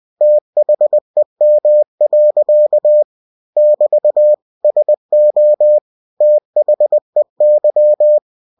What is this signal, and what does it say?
Morse code, heard as a keyed tone, about 20 wpm, 600 Hz. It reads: THEM. <BT> SO THEY
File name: cw-books / winnie-the-pooh / winnie-20wpm-page-0023